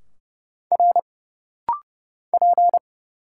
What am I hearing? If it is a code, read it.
Morse code, heard as a keyed tone, about 30 wpm, 700 Hz. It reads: R P
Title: Single Letter-Number Alt - Rapid-Fire 30wpm